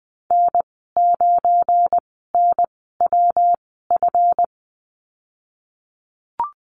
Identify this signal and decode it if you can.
Morse code, heard as a keyed tone, about 20 wpm, spaced as for 14 wpm, 700 Hz. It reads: N9NWF
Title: 200 Lesson 32g - Letter k - Call Signs 20wpm